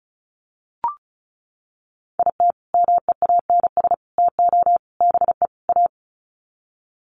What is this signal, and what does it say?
Morse code, heard as a keyed tone, about 34 wpm, 700 Hz. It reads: IT MEANS TO BE A
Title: Sets of 5 Words Encore2 35wpm